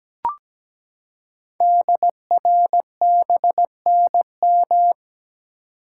Morse code, heard as a keyed tone, about 17 wpm, 700 Hz. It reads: DRBNM
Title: Code Groups - 5 Letters - 17wpm